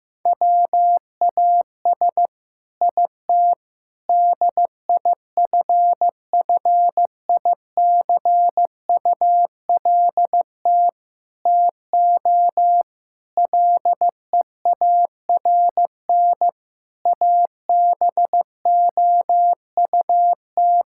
Morse code, heard as a keyed tone, about 15 wpm, 700 Hz. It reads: WAS IT DIFFICULT TO LEARN ABOUT
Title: Sentences from Top 1000 Words - Review Part 6 - 15wpm